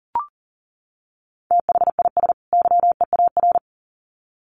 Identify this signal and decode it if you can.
Morse code, heard as a keyed tone, about 40 wpm, 700 Hz. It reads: THIS YEAR
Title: Sets of 2 Words 40wpm